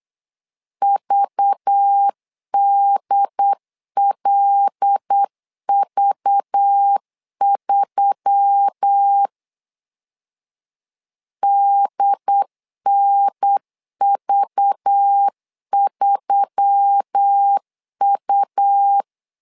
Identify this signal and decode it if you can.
Morse code, heard as a keyed tone, about 8 wpm, 800 Hz. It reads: VDLV3 DNV3U
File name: LEZIONI / 5 LBVC / 5 LBV 8wpm 2837